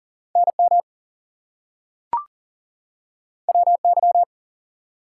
Morse code, heard as a keyed tone, about 40 wpm, 700 Hz. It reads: NM WY
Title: US States and Canadian Provinces 40wpm